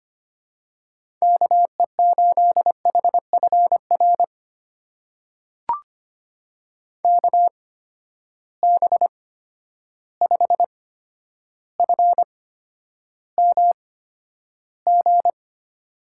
Morse code, heard as a keyed tone, about 25 wpm, spaced as for 11 wpm, 700 Hz. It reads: KE8HFR KB5FMG
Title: USA Call Signs - 6 Characters - 8x Character Spacing - 25wpm